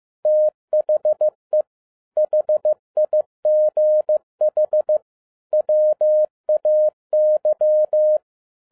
Morse code, heard as a keyed tone, about 15 wpm, 610 Hz. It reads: THE HIGH WAY
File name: 2012-01-mp3 / n07-ARA JR0DIL-15wpm